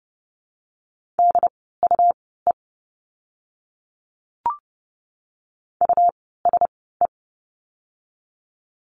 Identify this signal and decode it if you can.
Morse code, heard as a keyed tone, about 29 wpm, spaced as for 17 wpm, 700 Hz. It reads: DUE USE
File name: Lesson 17f - Letter u - Words 30wpm